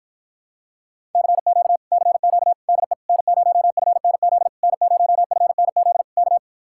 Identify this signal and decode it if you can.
Morse code, heard as a keyed tone, about 53 wpm, 700 Hz. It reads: CQ CQ DE N0FNZ N0FNZ K